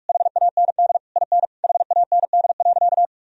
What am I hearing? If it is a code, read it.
Morse code, heard as a keyed tone, about 46 wpm, 700 Hz. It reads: HAND IN HAND.
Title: Sentences from Top 1000 Words - Review Part 3 - 45wpm